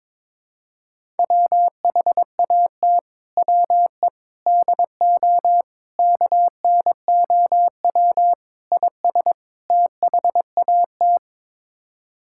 Morse code, heard as a keyed tone, about 21 wpm, 700 Hz. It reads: WHAT WE DO KNOW IS THAT
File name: practice / 22wpm / Sets of 6 Words - 22wpm